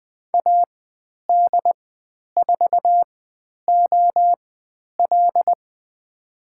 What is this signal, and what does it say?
Morse code, heard as a keyed tone, about 20 wpm, spaced as for 10 wpm, 700 Hz. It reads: AD4OL
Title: 102 Lesson 17g - Letter u - Call Signs 20wpm